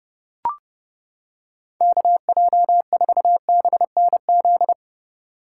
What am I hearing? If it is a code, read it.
Morse code, heard as a keyed tone, about 29 wpm, 700 Hz. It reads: KJ4BNZ